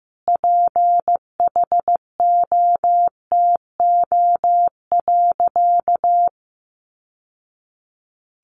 Morse code, heard as a keyed tone, about 15 wpm, 700 Hz. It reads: PHOTO.
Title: Sentences from Top 1200 Words 15wpm